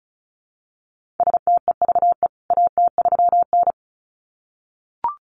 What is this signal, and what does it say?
Morse code, heard as a keyed tone, about 34 wpm, 700 Hz. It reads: STEVE AT3N
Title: CWT Exchange 35wpm